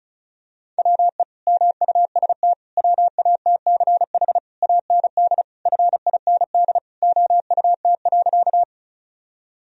Morse code, heard as a keyed tone, about 34 wpm, 700 Hz. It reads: WE MUST WATCH AND FIND OUT.
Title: Sentences from Top 1000 Words - Review Part 1 - 35wpm